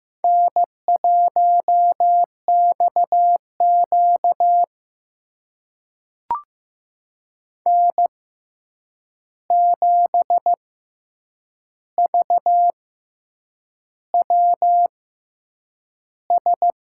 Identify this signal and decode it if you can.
Morse code, heard as a keyed tone, about 15 wpm, spaced as for 7 wpm, 700 Hz. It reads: N1XQ N7VWS
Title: USA Call Signs - 6x Character Spacing - 15wpm